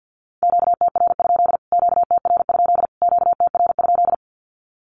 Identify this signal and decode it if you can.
Morse code, heard as a keyed tone, about 50 wpm, 700 Hz. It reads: QTR? QTR? QTR?
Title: Q-Codes Repeated 3x 50wpm